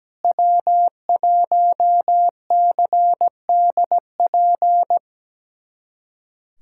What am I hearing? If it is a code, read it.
Morse code, heard as a keyed tone, about 17 wpm, 700 Hz. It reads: W1CDP
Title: USA Callsign DE Callsign 17wpm